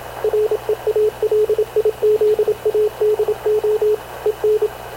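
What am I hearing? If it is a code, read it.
Morse code, heard as a keyed tone, about 27 wpm, 430 Hz, with radio noise. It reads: REALIZADO EN
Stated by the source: Havana Radio CLA press (PX) broadcast, recorded by Radio Officer Paul Dumesnil aboard mv CNAV QUEST/CZDO.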